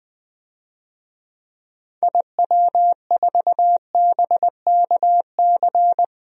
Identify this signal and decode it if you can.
Morse code, heard as a keyed tone, about 20 wpm, 700 Hz. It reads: IW4BKC
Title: DXCC-100 Call Signs - 6 Characters - Repeated 2x - 20wpm